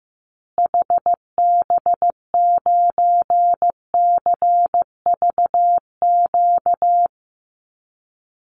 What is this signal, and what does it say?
Morse code, heard as a keyed tone, about 15 wpm, 700 Hz. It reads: HB9CVQ
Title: Call Signs Repeated 3x - 15wpm